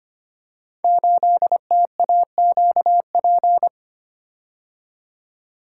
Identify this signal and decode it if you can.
Morse code, heard as a keyed tone, about 25 wpm, 700 Hz. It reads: 8TAQP
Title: Code Groups - 5 Letter:Numbers - 25wpm